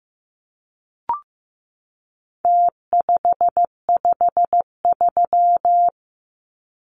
Morse code, heard as a keyed tone, about 15 wpm, 700 Hz. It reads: T553
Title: Numbers - Contest Style - 4-Digit 15wpm